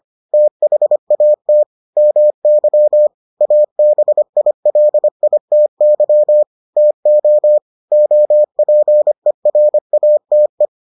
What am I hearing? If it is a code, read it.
Morse code, heard as a keyed tone, about 25 wpm, 600 Hz. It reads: THAT MY ABILITY TO OPERATE